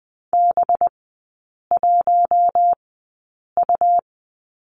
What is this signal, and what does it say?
Morse code, heard as a keyed tone, about 20 wpm, spaced as for 8 wpm, 700 Hz. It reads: B1U
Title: 198 Lesson 32e - Letter k - Three Characters 20wpm